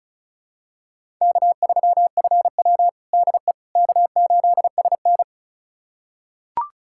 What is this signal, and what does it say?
Morse code, heard as a keyed tone, about 36 wpm, 700 Hz. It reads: K3FW DE K8SN
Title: USA Callsign DE Callsign - 4 Characters - 35wpm